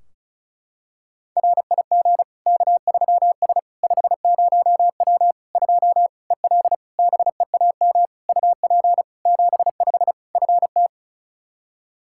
Morse code, heard as a keyed tone, about 36 wpm, 700 Hz. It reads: RIG K3S 50W 2 EL BEAM UP 75 FT
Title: Practice QSO - Exchange 2 - Info - 35wpm